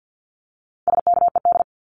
Morse code, heard as a keyed tone, about 48 wpm, 700 Hz. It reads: SKED
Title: Abbreviations Repeated 3x - 50wpm